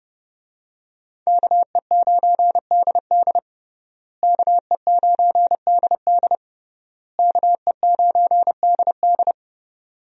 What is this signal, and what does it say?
Morse code, heard as a keyed tone, about 32 wpm, 700 Hz. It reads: KE9DD KE9DD KE9DD
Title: USA Call Signs - 5 Characters - Repeated 3x - 30wpm